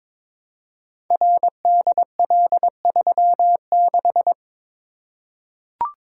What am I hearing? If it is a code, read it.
Morse code, heard as a keyed tone, about 22 wpm, 700 Hz. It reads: RDL36
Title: Code Groups - 5 Letter:Numbers - 22wpm